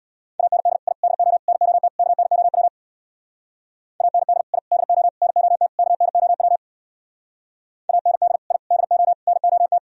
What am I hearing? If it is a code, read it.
Morse code, heard as a keyed tone, about 64 wpm, 690 Hz. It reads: AND I DO NOT KNOW AND I DO NOT KNOW AND I DO NOT